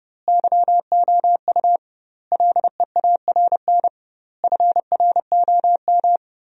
Morse code, heard as a keyed tone, about 30 wpm, 700 Hz. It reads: YOU LEARN FROM
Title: 2x Word-Spacing - Sentences from Top 300 Words - 30wpm